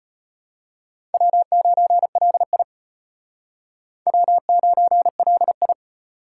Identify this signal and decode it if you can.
Morse code, heard as a keyed tone, about 33 wpm, 700 Hz. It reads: W9LI W9LI